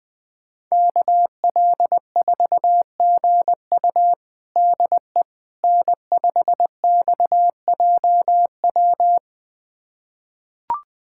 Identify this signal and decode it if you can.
Morse code, heard as a keyed tone, about 20 wpm, 700 Hz. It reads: KL4GU DE N5XJW
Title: USA Callsign DE Callsign - 5 Characters - 20wpm